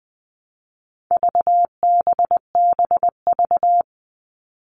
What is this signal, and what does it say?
Morse code, heard as a keyed tone, about 20 wpm, 700 Hz. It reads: VBBV
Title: All Cut Numbers - 4-Digit 20wpm